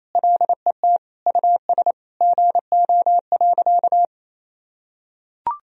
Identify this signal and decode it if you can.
Morse code, heard as a keyed tone, about 28 wpm, 700 Hz. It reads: LET US GO.
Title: Sentences from Top 400 Words and Abbreviations 28wpm